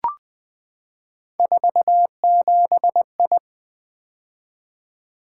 Morse code, heard as a keyed tone, about 20 wpm, 700 Hz. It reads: 47I
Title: Three Letter-Number Combinations - 20wpm